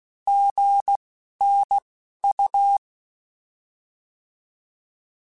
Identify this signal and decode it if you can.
Morse code, heard as a keyed tone, about 16 wpm, spaced as for 11 wpm, 800 Hz. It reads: GNU
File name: Alphabetical / Alphabetical mp3s / Gs @10